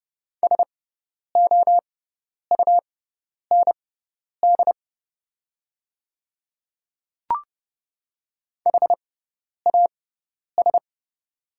Morse code, heard as a keyed tone, about 30 wpm, spaced as for 10 wpm, 700 Hz. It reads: SOUND HAS
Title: 6x Character Spacing - Top 300 Words - 30wpm